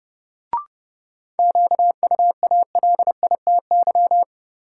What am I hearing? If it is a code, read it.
Morse code, heard as a keyed tone, about 31 wpm, 700 Hz. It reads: QUALITY
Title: Sentences from Top 1000 Words - Review Part 4 - New Word Drill - 30wpm